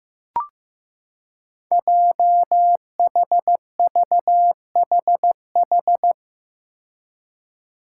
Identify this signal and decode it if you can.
Morse code, heard as a keyed tone, about 15 wpm, 700 Hz. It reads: JHVHH
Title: Five Challenging Letter-Number Combinations 15wpm